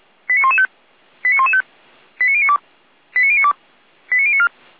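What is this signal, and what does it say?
Five-tone selective call (ZVEI), heard as quick runs of five tones, tones of about 70 ms. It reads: 89115 89115 89092 89092 89094
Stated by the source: Five-tone ZVEI selective call received by SDR.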